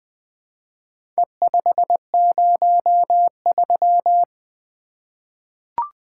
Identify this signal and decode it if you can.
Morse code, heard as a keyed tone, about 20 wpm, 700 Hz. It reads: E503
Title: Four Letter-Number Combinations - 20wpm